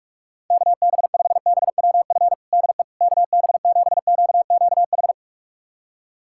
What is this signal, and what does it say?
Morse code, heard as a keyed tone, about 47 wpm, 700 Hz. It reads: KB5BWF DE KB7QQH
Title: USA Callsign DE Callsign 45wpm